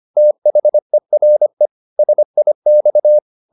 Morse code, heard as a keyed tone, about 25 wpm, 600 Hz. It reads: THERE SIX
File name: the old man and the sea part 1 25wpm